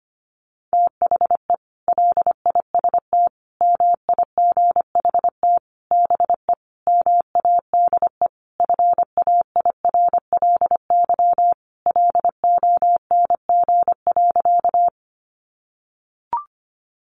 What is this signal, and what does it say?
Morse code, heard as a keyed tone, about 25 wpm, 700 Hz. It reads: THE LIST MIGHT BE MADE FAIRLY LONG.
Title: Sentences from Top 1100 Words 25wpm